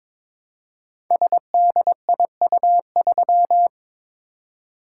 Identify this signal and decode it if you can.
Morse code, heard as a keyed tone, about 22 wpm, 700 Hz. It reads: SDIU3
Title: Code Groups - 5 Letter:Numbers - 22wpm